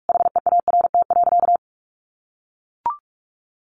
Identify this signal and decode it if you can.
Morse code, heard as a keyed tone, about 44 wpm, 700 Hz. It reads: HEART.
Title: Sentences from Top 1500 Words 45wpm